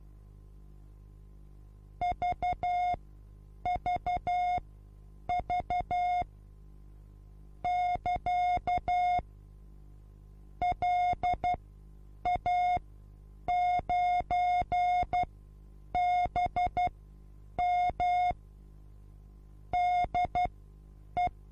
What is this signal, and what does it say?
Morse code, heard as a keyed tone, about 12 wpm, spaced as for 8 wpm, 720 Hz. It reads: VVV <KA> LA9BM DE
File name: Onlinelehrgang / morsen / DJ4UF Morsen23